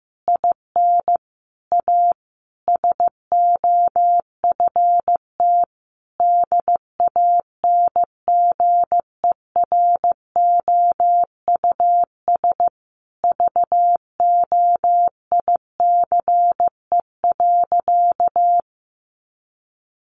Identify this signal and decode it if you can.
Morse code, heard as a keyed tone, about 14 wpm, 700 Hz. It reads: IN A SOFT DANGEROUS VOICE.
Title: Sentences from Top 1500 Words 15wpm